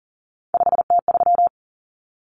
Morse code, heard as a keyed tone, about 40 wpm, 700 Hz. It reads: 5T3
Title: Common Cut Numbers - 3-Digit 40wpm